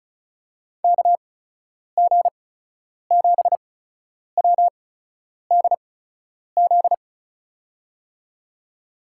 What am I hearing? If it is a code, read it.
Morse code, heard as a keyed tone, about 36 wpm, spaced as for 10 wpm, 700 Hz. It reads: KG7WDZ
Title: USA Call Signs - 6 Characters - 8x Character Spacing - 35wpm